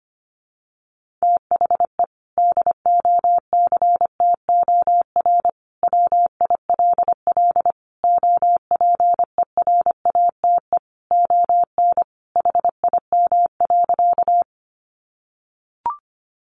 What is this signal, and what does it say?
Morse code, heard as a keyed tone, about 25 wpm, 700 Hz. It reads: THE DOCTOR WILL OPERATE ON HIM.
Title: Sentences from Top 1500 Words 25wpm